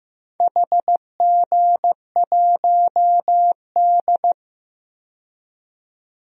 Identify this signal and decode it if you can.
Morse code, heard as a keyed tone, about 15 wpm, 700 Hz. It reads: HG1D
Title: DXCC-100 Call Signs - 4 Characters - 15wpm